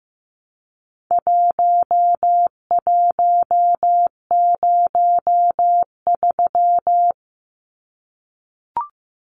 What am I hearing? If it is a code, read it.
Morse code, heard as a keyed tone, about 15 wpm, 700 Hz. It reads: 1103
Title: Numbers - Contest Style - 4-Digit 15wpm